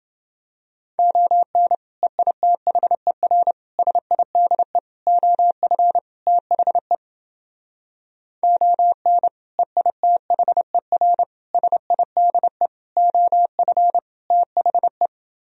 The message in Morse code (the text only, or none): ON EITHER SIDE OF THE ON EITHER SIDE OF THE